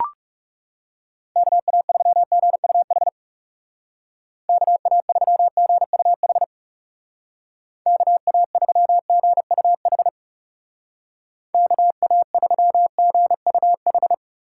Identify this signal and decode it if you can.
Morse code, heard as a keyed tone, about 36 wpm, 700 Hz. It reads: KA3GUH KA3GUH KA3GUH KA3GUH
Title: Speed Racing - USA 2x3 Call Signs 30wpm